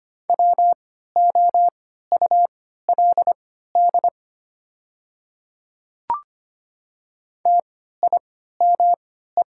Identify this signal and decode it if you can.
Morse code, heard as a keyed tone, about 25 wpm, spaced as for 14 wpm, 700 Hz. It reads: WOULD TIME
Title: 3x Character Spacing - Top 100 Words - 25wpm